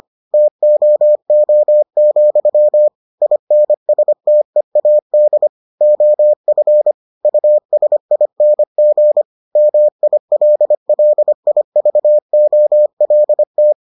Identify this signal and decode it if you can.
Morse code, heard as a keyed tone, about 25 wpm, 600 Hz. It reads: TOO, INSTEAD OF USING MILLIVOLT